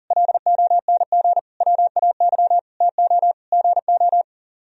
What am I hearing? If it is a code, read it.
Morse code, heard as a keyed tone, about 49 wpm, 700 Hz. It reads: LONG WAY TO GO
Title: Speed-Racing - Sets of 5 Words 40wpm